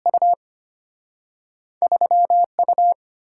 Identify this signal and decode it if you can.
Morse code, heard as a keyed tone, about 35 wpm, spaced as for 30 wpm, 700 Hz. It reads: U 3U
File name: practice / 25wpm / Speed-Racing - Two Letter-Number Combinations - 25wpm